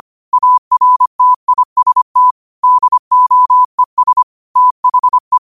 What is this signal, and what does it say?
Morse code, heard as a keyed tone, about 25 wpm, 1000 Hz. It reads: ARTIST DOES THE